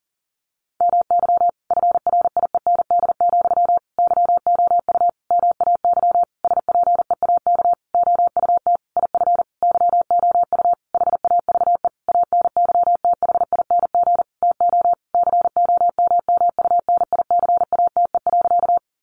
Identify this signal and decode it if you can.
Morse code, heard as a keyed tone, about 40 wpm, 700 Hz. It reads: MY FRIEND, YOU MAY SPEAK OUT IF YOU HAVE ANYTHING TO COMMUNICATE.
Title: Sentences from Top 1300 Words 40wpm